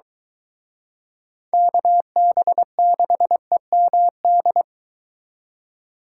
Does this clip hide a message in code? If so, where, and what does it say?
Yes: Morse code from 1.5 to 4.6 s, heard as a keyed tone, about 23 wpm, 700 Hz. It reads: KB6EMD